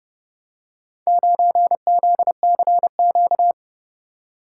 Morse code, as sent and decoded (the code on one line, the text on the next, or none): ----. --.. -.-. --.-
9ZCQ